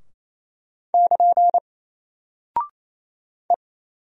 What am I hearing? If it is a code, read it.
Morse code, heard as a keyed tone, about 27 wpm, 700 Hz. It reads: <KN> E
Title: Single Character (Letter:Number:Punctuation:Prosign) 28wpm